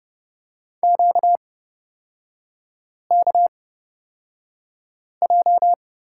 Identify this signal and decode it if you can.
Morse code, heard as a keyed tone, about 31 wpm, spaced as for 5 wpm, 690 Hz. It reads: QKJ
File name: Lesson 38e - Letter q - Three Characters 30wpm